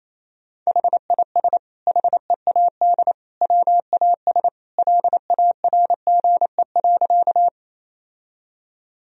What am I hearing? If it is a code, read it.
Morse code, heard as a keyed tone, about 28 wpm, 700 Hz. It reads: HIS HEAD WAS LARGE.
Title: Sentences from Top 1000 Words - Review Part 1 - 28wpm